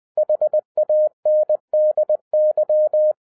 Morse code, heard as a keyed tone, about 20 wpm, 610 Hz. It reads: HANDY
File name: 2012-01-mp3 / n05-CAME BACK TO HAM RADIO AFTER 30 YEARS OF INTERVAL-20wpm